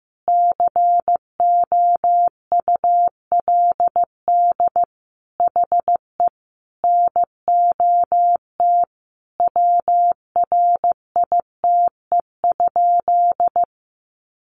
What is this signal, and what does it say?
Morse code, heard as a keyed tone, about 15 wpm, 700 Hz. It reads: COULD HE NOT WRITE?
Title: Sentences from Top 600 Words 15wpm